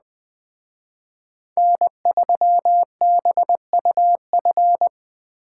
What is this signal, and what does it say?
Morse code, heard as a keyed tone, about 21 wpm, 700 Hz. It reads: N3BUF
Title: Speed Racing - USA Call Signs 17wpm